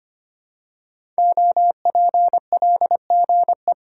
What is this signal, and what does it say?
Morse code, heard as a keyed tone, about 25 wpm, 700 Hz. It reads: OPLGE
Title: Code Groups - 5 Letters - 25wpm